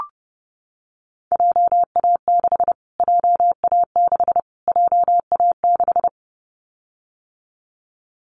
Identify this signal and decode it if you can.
Morse code, heard as a keyed tone, about 28 wpm, 700 Hz. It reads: JA6 JA6 JA6
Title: Callsign Prefixes for Top 25 DXCC - Repeated 3x - 30wpm